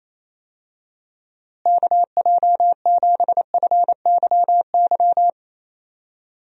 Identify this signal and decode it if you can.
Morse code, heard as a keyed tone, about 28 wpm, 700 Hz. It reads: KJ7FYY
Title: USA Call Signs - 6 Characters - Repeated 3x - 28wpm